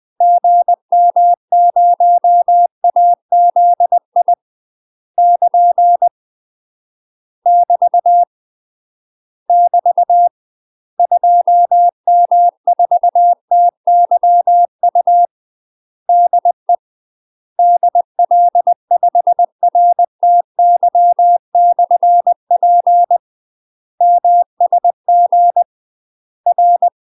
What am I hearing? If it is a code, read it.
Morse code, heard as a keyed tone, about 20 wpm, 700 Hz. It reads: GM0AZI <KN> <BT> <BT> 2M4TYU DE DL5RTY/P MSG R
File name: UK MASTER QSOs TRAINING / 20WPM / UK MASTER QSOs 0065